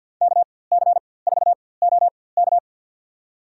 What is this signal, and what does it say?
Morse code, heard as a keyed tone, about 52 wpm, spaced as for 24 wpm, 700 Hz. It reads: KC4OK
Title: USA Call Signs - 5 Characters - 4x Character Spacing - 50wpm